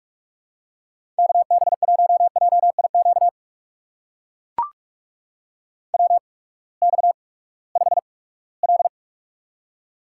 Morse code, heard as a keyed tone, about 44 wpm, spaced as for 22 wpm, 700 Hz. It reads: KB1JIQ WX5L
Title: USA Call Signs - 8x Character Spacing - 45wpm